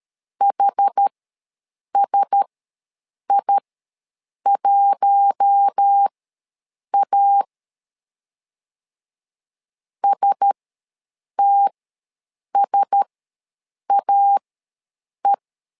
Morse code, heard as a keyed tone, about 13 wpm, spaced as for 7 wpm, 800 Hz. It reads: HSI1A STSAE
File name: LEZIONI / 4 Q1S ws 12wpm 1827